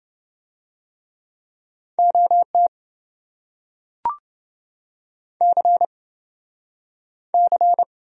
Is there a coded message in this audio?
Yes: Morse code from 2.0 to 7.8 s, heard as a keyed tone, about 28 wpm, 700 Hz. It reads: OT C C